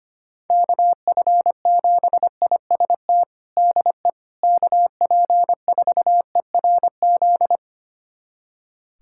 Morse code, heard as a keyed tone, about 25 wpm, 700 Hz. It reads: KF7IST DE KP4ERZ
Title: USA Callsign DE Callsign - 6 Characters - 25wpm